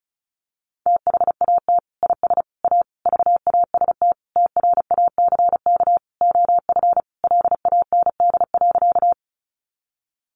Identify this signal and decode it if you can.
Morse code, heard as a keyed tone, about 34 wpm, 700 Hz. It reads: THAT IS A VAST TRACK OF LAND.
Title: Sentences from Top 1300 Words 35wpm